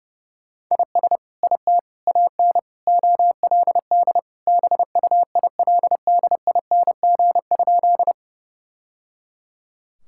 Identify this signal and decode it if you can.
Morse code, heard as a keyed tone, about 30 wpm, 700 Hz. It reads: IS IT AN OLD BUILDING?
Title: Sentences from Top 1000 Words - Review Part 5 - 30wpm